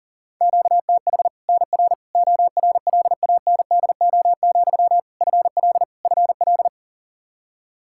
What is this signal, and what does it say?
Morse code, heard as a keyed tone, about 40 wpm, 700 Hz. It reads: QTH NR ORLANDO, FL FL
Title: Practice QSO - Exchange 1 - Beginning - 40wpm